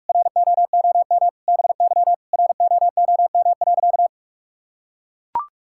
Morse code, heard as a keyed tone, about 41 wpm, 700 Hz. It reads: ROOM BY ROOM.
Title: Sentences from Top 1000 Words - Review Part 5 - 45wpm